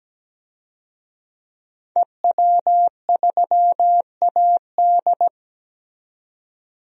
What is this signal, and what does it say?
Morse code, heard as a keyed tone, about 17 wpm, 700 Hz. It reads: EW3AD
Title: DXCC-100 Call Signs - 5 Characters - 17wpm